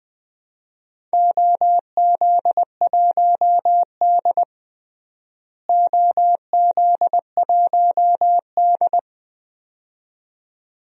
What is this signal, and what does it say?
Morse code, heard as a keyed tone, about 20 wpm, 700 Hz. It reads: OZ1D OZ1D
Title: DXCC-100 Call Signs - 4 Characters - Repeated 2x - 20wpm